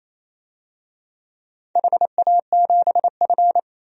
Morse code, heard as a keyed tone, about 27 wpm, 690 Hz. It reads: HA7F
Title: DXCC-100 Call Signs - 4 Characters - 28wpm